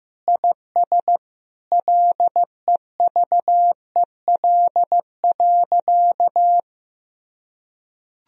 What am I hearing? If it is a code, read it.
Morse code, heard as a keyed tone, about 16 wpm, 700 Hz. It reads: IS LEVEL.
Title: Sentences from Top 1000 Words - Review Part 4 - 15wpm